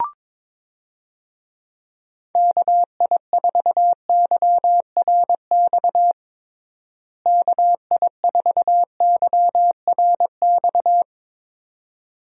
Morse code, heard as a keyed tone, about 22 wpm, 700 Hz. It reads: KI4YRX KI4YRX
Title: USA Call Signs - 6 Characters - Repeated 2x - 22wpm